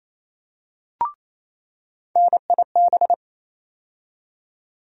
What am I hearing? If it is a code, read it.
Morse code, heard as a keyed tone, about 28 wpm, 700 Hz. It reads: NIB
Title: USA Call Sign Postfixes - 3 Letters 28wpm